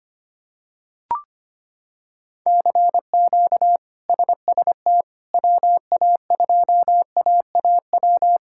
Morse code, heard as a keyed tone, about 25 wpm, 700 Hz. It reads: CQ SST WA2AAW
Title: K1USN SST QSO Practice - 25wpm